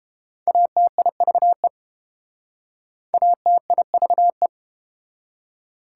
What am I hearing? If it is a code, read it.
Morse code, heard as a keyed tone, about 30 wpm, 700 Hz. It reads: ATIVE ATIVE